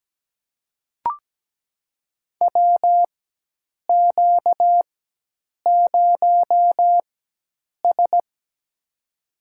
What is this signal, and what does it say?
Morse code, heard as a keyed tone, about 17 wpm, spaced as for 8 wpm, 700 Hz. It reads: WQ0S
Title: USA Call Signs - 4 Characters - 4x Character Spacing - 17wpm